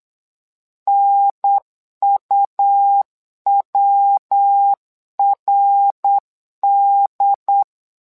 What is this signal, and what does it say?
Morse code, heard as a keyed tone, about 8 wpm, 800 Hz. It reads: NUWRD
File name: LEZIONI / 3 MWRO / 3 MWRO 8wpm 2606